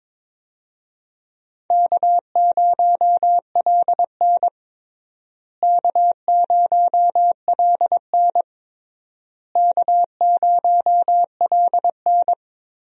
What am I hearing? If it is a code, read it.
Morse code, heard as a keyed tone, about 22 wpm, 700 Hz. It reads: K0LN K0LN K0LN
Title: USA Call Signs - 4 Characters - Repeated 3x - 22wpm